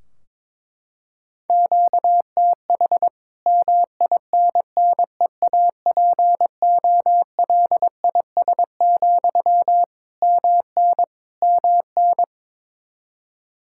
Morse code, heard as a keyed tone, about 22 wpm, 700 Hz. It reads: QTH MINNEAPOLIS, MN MN
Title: Practice QSO - Exchange 1 - Beginning - 22wpm